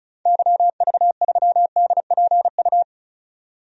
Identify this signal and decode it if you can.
Morse code, heard as a keyed tone, about 34 wpm, 700 Hz. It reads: YV3DPU
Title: DXCC-100 Call Signs - 6 Characters - 35wpm